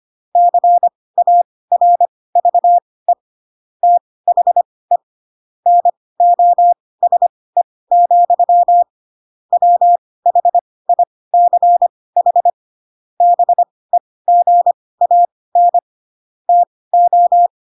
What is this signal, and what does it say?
Morse code, heard as a keyed tone, about 25 wpm, spaced as for 18 wpm, 700 Hz. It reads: CARVE THE NOSE, WHICH BEGAN TO
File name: Pinocchio ch3 12 25 ik1hge